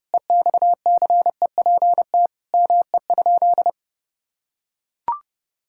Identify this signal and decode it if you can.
Morse code, heard as a keyed tone, about 30 wpm, 700 Hz. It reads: EXCEPT ME?
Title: Sentences from Top 1600 Words 30wpm